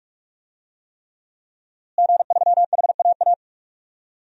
Morse code, heard as a keyed tone, about 46 wpm, 700 Hz. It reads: G3HAA
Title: DXCC-100 Call Signs - 5 Characters - Repeated 3x - 45wpm